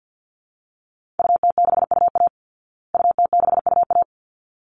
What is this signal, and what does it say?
Morse code, heard as a keyed tone, about 49 wpm, 700 Hz. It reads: UT6UA UT6UA
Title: Call Signs Repeated 3x - 50wpm